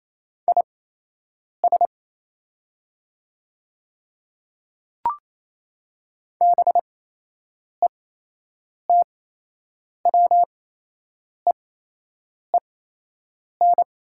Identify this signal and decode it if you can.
Morse code, heard as a keyed tone, about 28 wpm, spaced as for 8 wpm, 690 Hz. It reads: IS BETWEEN
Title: 8x Character Spacing - Top 300 Words - 28wpm